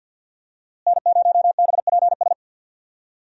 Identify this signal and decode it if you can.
Morse code, heard as a keyed tone, about 49 wpm, 700 Hz. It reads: N0BPS
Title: USA Call Signs - 5 Characters - Repeated 2x - 50wpm